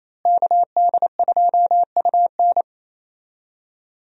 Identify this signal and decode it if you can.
Morse code, heard as a keyed tone, about 28 wpm, 700 Hz. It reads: KD2UN
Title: USA Call Signs - 5 Characters - Repeated 3x - 28wpm